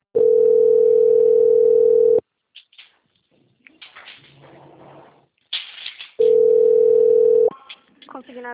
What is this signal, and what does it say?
Ringback tone (North America), 440 + 480 Hz, with speech.
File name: out- - - - - .44036